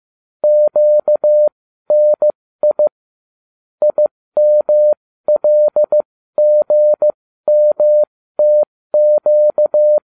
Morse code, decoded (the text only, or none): QNI IMLGMTQ